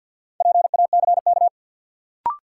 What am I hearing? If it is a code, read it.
Morse code, heard as a keyed tone, about 47 wpm, 700 Hz. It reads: PACK
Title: Sentences from Top 1300 Words - New Word Drill - 50wpm